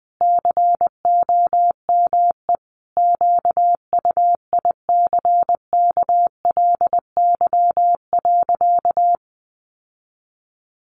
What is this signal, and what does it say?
Morse code, heard as a keyed tone, about 20 wpm, 700 Hz. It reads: COME QUICKLY.
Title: Sentences from Top 700 Words 20wpm